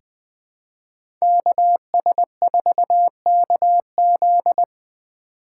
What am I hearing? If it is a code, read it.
Morse code, heard as a keyed tone, about 20 wpm, 700 Hz. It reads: KS4KZ
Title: USA Call Signs - 5 Characters - 4x Character Spacing - 20wpm